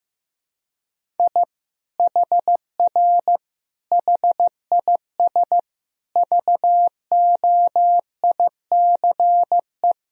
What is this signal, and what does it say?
Morse code, heard as a keyed tone, about 15 wpm, 700 Hz. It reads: I HR HIS VOICE
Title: Sentences from Top 500 Words and Abbreviations 15wpm